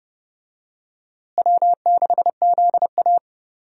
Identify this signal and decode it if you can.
Morse code, heard as a keyed tone, about 32 wpm, 700 Hz. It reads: W6ZA
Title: USA Call Signs - 4 Characters - Repeated 2x - 30wpm